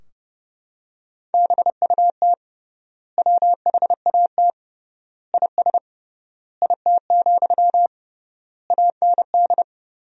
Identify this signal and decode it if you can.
Morse code, heard as a keyed tone, about 30 wpm, 700 Hz. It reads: BUT WHAT IS IT, AND
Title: 3x Word-Spacing - Sentences from Top 300 Words - 30wpm